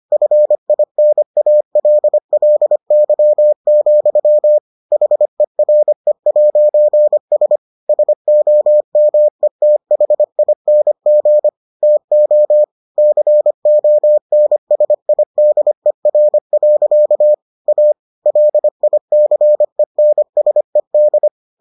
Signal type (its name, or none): Morse code